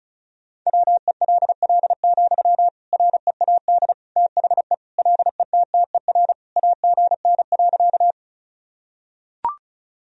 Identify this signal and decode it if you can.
Morse code, heard as a keyed tone, about 36 wpm, 700 Hz. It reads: WELL, READ THE LETTER AGN.
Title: Sentences from Top 300 Words and Abbreviations 35wpm